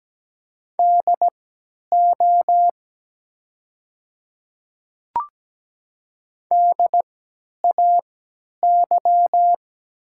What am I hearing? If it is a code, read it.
Morse code, heard as a keyed tone, about 17 wpm, spaced as for 10 wpm, 700 Hz. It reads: DO DAY
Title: 3x Character Spacing - Top 500 Words - 17wpm